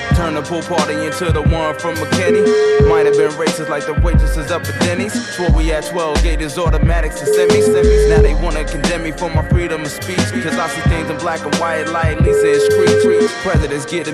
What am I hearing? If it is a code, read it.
Ringback tone (Europe, 425 Hz), with speech.